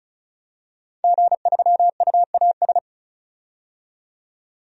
Morse code, heard as a keyed tone, about 36 wpm, 700 Hz. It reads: G3UAS
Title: DXCC-100 Call Signs - 5 Characters - 35wpm